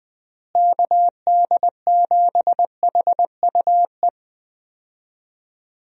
Morse code, heard as a keyed tone, about 20 wpm, 700 Hz. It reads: KD7HUE
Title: USA Call Signs - 6 Characters - Repeated 3x - 20wpm